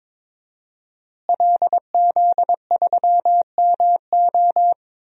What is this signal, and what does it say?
Morse code, heard as a keyed tone, about 22 wpm, 700 Hz. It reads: LZ3MO
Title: DXCC-100 Call Signs - 5 Characters - Repeated 2x - 22wpm